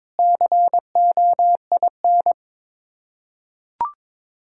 Morse code, heard as a keyed tone, about 22 wpm, 700 Hz. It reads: COIN
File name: practice / 22wpm / Sets of 7 Words - 22wpm